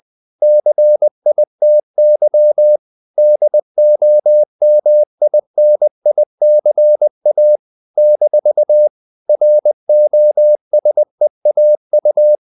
Morse code, heard as a keyed tone, about 20 wpm, 600 Hz. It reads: CITY DOMINICA - ROSEAU
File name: CW / Practice / NationalCapitals20